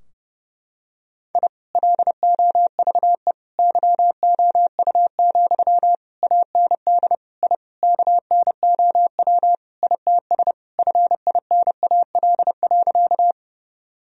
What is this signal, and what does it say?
Morse code, heard as a keyed tone, about 30 wpm, 700 Hz. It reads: I LOVE YOU, AND I KNOW ITS FINAL.
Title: Sentences from Top 1000 Words - Review Part 2 - 30wpm